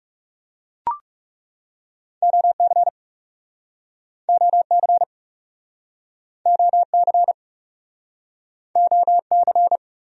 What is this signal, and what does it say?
Morse code, heard as a keyed tone, about 44 wpm, 700 Hz. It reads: OC OC OC OC